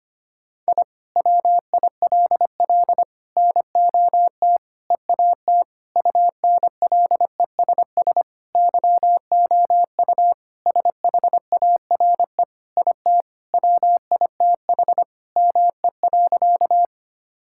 Morse code, heard as a keyed tone, about 25 wpm, 700 Hz. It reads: I WILL NOT EAT UNLESS YOU SHARE IT WITH ME.
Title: Sentences from Top 1000 Words - Review Part 6 - 25wpm